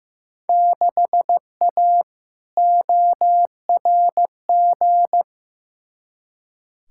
Morse code, heard as a keyed tone, about 15 wpm, 700 Hz. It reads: 6A ORG
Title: Field Day Exchanges Encore - 15wpm